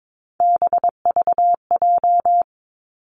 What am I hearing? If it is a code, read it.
Morse code, heard as a keyed tone, about 22 wpm, 700 Hz. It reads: BVJ
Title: Three Challenging Letter-Number Combinations 22wpm